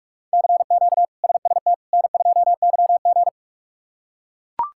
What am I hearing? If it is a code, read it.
Morse code, heard as a keyed tone, about 42 wpm, 700 Hz. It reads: CQ SST N2YG
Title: K1USN SST QSO Practice - 45wpm